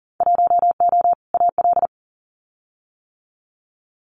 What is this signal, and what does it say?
Morse code, heard as a keyed tone, about 39 wpm, 700 Hz. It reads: 1O AL
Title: Winter Field Day Exchanges 40wpm